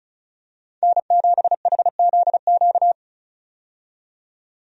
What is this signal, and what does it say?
Morse code, heard as a keyed tone, about 36 wpm, 700 Hz. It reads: N7HZQ
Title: USA Call Signs - 5 Characters - 35wpm